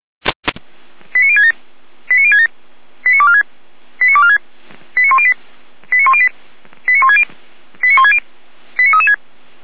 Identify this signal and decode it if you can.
Five-tone selective call (ZVEI), heard as quick runs of five tones, tones of about 70 ms. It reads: 89067 89067 89246 89246 89108 89108 89160 89160 89336